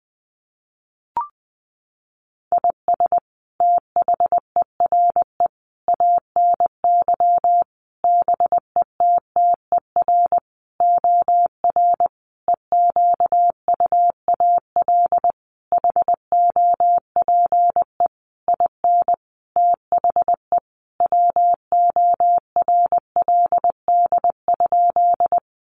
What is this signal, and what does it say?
Morse code, heard as a keyed tone, about 20 wpm, 700 Hz. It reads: IS THERE ANY BETTER OR EQUAL HOPE IN THE WORLD?
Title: Sentences from Top 1400 Words 20wpm